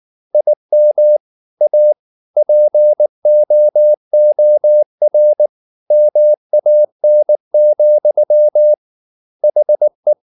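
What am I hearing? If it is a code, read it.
Morse code, heard as a keyed tone, about 19 wpm, 600 Hz. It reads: IM A POOR MAN, HE
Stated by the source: Morse code at 19 wpm from CW med Gnister, reading Ken Follett's novel Never in English.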